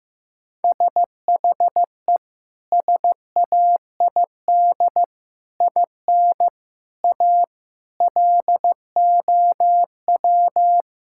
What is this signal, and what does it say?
Morse code, heard as a keyed tone, about 15 wpm, 700 Hz. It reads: SHE SAID IN A LOW
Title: Sentences from Top 1600 Words 15wpm